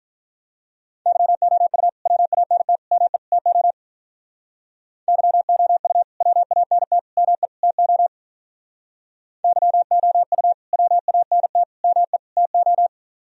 Morse code, heard as a keyed tone, about 46 wpm, 700 Hz. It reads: YOU WANT ME TO YOU WANT ME TO YOU WANT ME TO